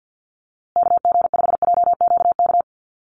Morse code, heard as a keyed tone, about 50 wpm, 700 Hz. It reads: KG5JQK
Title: USA Call Signs Repeated 2x 50wpm